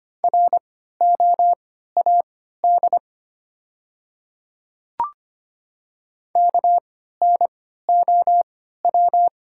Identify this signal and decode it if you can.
Morse code, heard as a keyed tone, about 25 wpm, spaced as for 14 wpm, 700 Hz. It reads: ROAD KNOW
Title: 3x Character Spacing - Top 500 Words - 25wpm